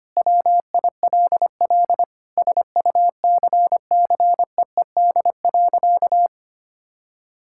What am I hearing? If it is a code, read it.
Morse code, heard as a keyed tone, about 25 wpm, 700 Hz. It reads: WILL SUCCEED.
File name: practice / 25wpm / Sentences with 9-Word Phrases - 25wpm